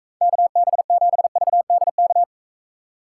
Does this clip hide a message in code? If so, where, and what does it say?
Yes: Morse code from 0.2 to 2.2 s, heard as a keyed tone, about 45 wpm, 700 Hz. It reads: KB7VDK